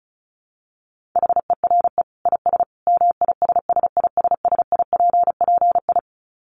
Morse code, heard as a keyed tone, about 34 wpm, 700 Hz. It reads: HERE IS MISSISSIPPI
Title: Dit Challenge 35wpm